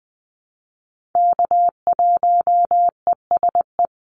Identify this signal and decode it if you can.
Morse code, heard as a keyed tone, about 20 wpm, 700 Hz. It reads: K1ESE
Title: Call Signs - 20wpm